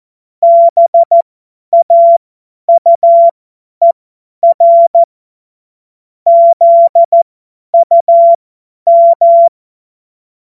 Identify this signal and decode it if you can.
Morse code, heard as a keyed tone, about 14 wpm, spaced as for 10 wpm, 680 Hz. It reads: BAUER ZUM
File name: Witz-09-September-2025-50-F70